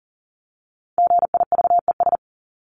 Morse code, heard as a keyed tone, about 40 wpm, 700 Hz. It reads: GIVES
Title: ES:S Suffix - Top 100 Words with Base Words 40wpm